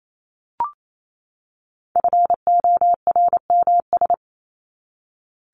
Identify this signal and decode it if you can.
Morse code, heard as a keyed tone, about 28 wpm, 700 Hz. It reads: FORMS
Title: ES:S Suffix - Top 100 Words with Base Words 28wpm